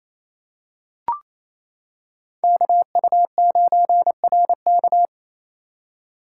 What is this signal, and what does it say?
Morse code, heard as a keyed tone, about 28 wpm, 700 Hz. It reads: KU9RK
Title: K1USN SST QSO Practice - 28wpm